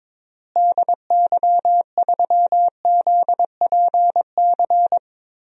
Morse code, heard as a keyed tone, about 21 wpm, 700 Hz. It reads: DY3ZPC
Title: DXCC-100 Call Signs - 6 Characters - Repeated 3x - 22wpm